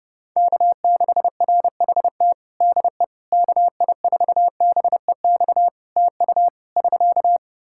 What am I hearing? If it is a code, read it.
Morse code, heard as a keyed tone, about 32 wpm, 700 Hz. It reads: K6RHT DE KI4BEX TU <SK>
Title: K1USN SST QSO Practice 30wpm